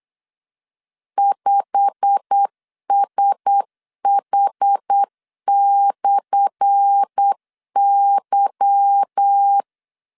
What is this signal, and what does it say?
Morse code, heard as a keyed tone, about 8 wpm, 800 Hz. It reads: 5SH/Y
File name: LEZIONI / 10 SIGNS / 10A SLASH 8wpm 3452